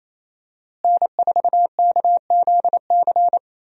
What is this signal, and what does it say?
Morse code, heard as a keyed tone, about 28 wpm, 700 Hz. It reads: N4KZC
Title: USA Call Signs - 5 Characters - Repeated 3x - 28wpm